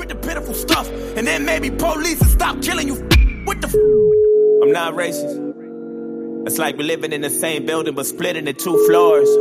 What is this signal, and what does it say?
Ringback tone (Europe, 425 Hz), with speech.